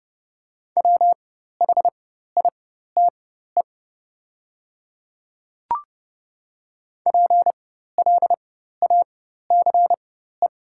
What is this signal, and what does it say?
Morse code, heard as a keyed tone, about 29 wpm, spaced as for 14 wpm, 700 Hz. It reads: WHITE PLACE
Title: 4x Character Spacing - Top 300 Words - 30wpm